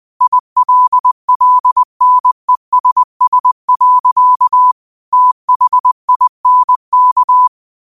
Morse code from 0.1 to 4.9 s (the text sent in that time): ILLNESS.